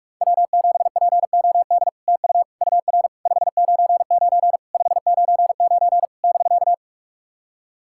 Morse code, heard as a keyed tone, about 44 wpm, 700 Hz. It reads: W7POD TU UR 599 599 <BK>
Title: SOTA QSO Practice 45wpm